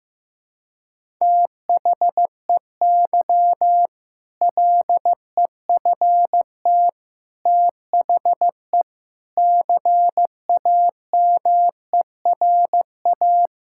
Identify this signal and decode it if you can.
Morse code, heard as a keyed tone, about 15 wpm, 700 Hz. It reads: THEY LEFT THE CAMERA
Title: Sentences from Top 1000 Words - Review Part 5 - 15wpm